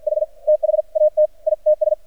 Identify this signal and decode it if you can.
Morse code, heard as a keyed tone, about 48 wpm, spaced as for 41 wpm, 610 Hz. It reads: H THAT ITS
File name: archived / CRSnet